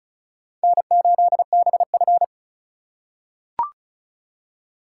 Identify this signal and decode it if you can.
Morse code, heard as a keyed tone, about 36 wpm, 700 Hz. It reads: N8BF
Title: USA Call Signs - 4 Characters - 3x Character Spacing - 35wpm